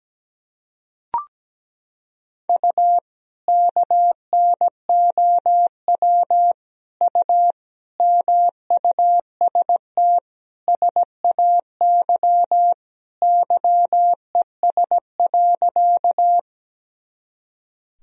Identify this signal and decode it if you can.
Morse code, heard as a keyed tone, about 17 wpm, 700 Hz. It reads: U KNOW U MUST SAY YES.
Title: Sentences from Top 500 Words and Abbreviations 17wpm